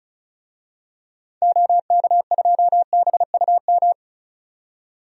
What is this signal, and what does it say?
Morse code, heard as a keyed tone, about 36 wpm, 700 Hz. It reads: OK2BUM
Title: DXCC-100 Call Signs - 6 Characters - 35wpm